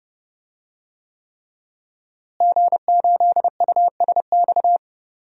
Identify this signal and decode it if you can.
Morse code, heard as a keyed tone, about 28 wpm, 700 Hz. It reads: G8USX
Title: DXCC-100 Call Signs - 5 Characters - Repeated 3x - 30wpm